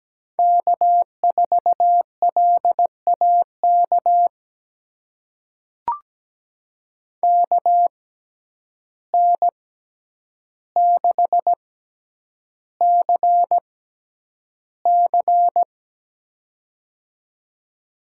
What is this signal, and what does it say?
Morse code, heard as a keyed tone, about 17 wpm, spaced as for 9 wpm, 700 Hz. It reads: K4LAK KN6CC
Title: USA Call Signs - 5 Characters - 6x Character Spacing - 17wpm